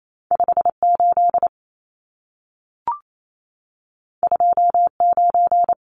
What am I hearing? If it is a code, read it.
Morse code, heard as a keyed tone, about 28 wpm, 700 Hz. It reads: 58 29
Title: Numbers - 2-Digit 28wpm